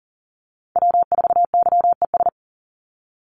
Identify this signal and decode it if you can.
Morse code, heard as a keyed tone, about 40 wpm, 700 Hz. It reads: W4YES
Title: Call Signs - 40wpm